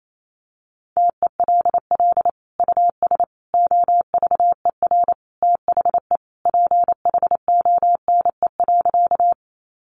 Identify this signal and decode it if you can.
Morse code, heard as a keyed tone, about 28 wpm, 700 Hz. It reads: TELL US OVER THE PHONE.
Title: Sentences from Top 1100 Words 28wpm